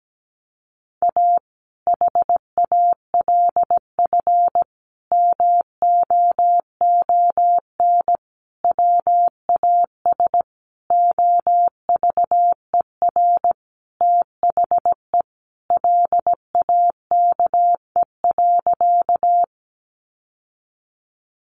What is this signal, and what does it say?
Morse code, heard as a keyed tone, about 17 wpm, 700 Hz. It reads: A HALF MOON WAS OVER THE LAKE.
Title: Sentences from Top 1100 Words 17wpm